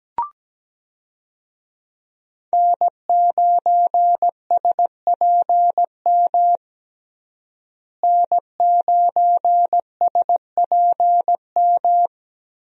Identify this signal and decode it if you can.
Morse code, heard as a keyed tone, about 17 wpm, 700 Hz. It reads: N9SPM N9SPM
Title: USA Call Signs - 5 Characters - Repeated 2x - 17wpm